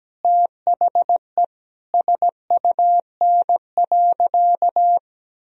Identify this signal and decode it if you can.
Morse code, heard as a keyed tone, about 17 wpm, 700 Hz. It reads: THE SUN.
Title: Sentences from Top 1000 Words - Review Part 1 - 17wpm